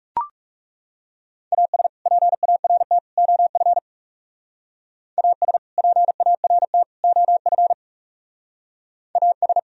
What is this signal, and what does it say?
Morse code, heard as a keyed tone, about 38 wpm, 700 Hz. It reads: AS PART OF AS PART OF AS